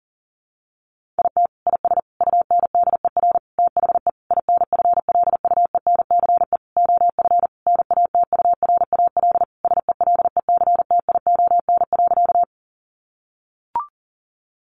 Morse code, heard as a keyed tone, about 40 wpm, 700 Hz. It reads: IT IS UNDER THE INFLUENCE OF NATURAL SELECTION.
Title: Sentences from Top 1400 Words 40wpm